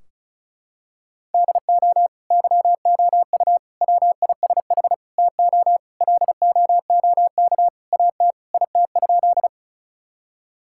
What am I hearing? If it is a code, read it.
Morse code, heard as a keyed tone, about 34 wpm, 700 Hz. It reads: DO YOU WISH TO LOOK AT IT?
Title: Sentences from Top 1600 Words 35wpm